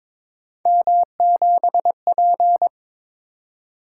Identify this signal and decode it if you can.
Morse code, heard as a keyed tone, about 22 wpm, 700 Hz. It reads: M7P
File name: practice / 22wpm / Three Letter-Number Combinations - 22wpm